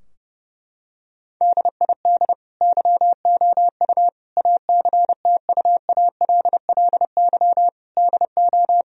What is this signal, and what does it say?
Morse code, heard as a keyed tone, about 32 wpm, 700 Hz. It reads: DID YOU ACTUALLY DO
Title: Sentences from Top 1000 Words - Review Part 4 - 30wpm